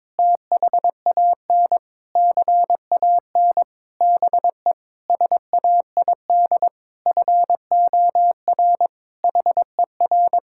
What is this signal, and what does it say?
Morse code, heard as a keyed tone, about 21 wpm, 700 Hz. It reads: THAN CAN BE SAID FOR HER
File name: practice / 22wpm / Sentences with 8-Word Phrases - 22wpm